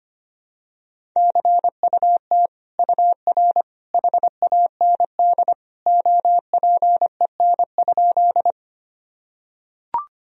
Morse code, heard as a keyed tone, about 25 wpm, 700 Hz. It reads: CUT UR HAND OPEN?
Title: Sentences from Top 300 Words and Abbreviations 25wpm